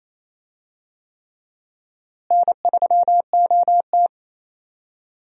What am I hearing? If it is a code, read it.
Morse code, heard as a keyed tone, about 28 wpm, 700 Hz. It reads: N3OT
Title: USA Call Signs - 4 Characters - Repeated 3x - 28wpm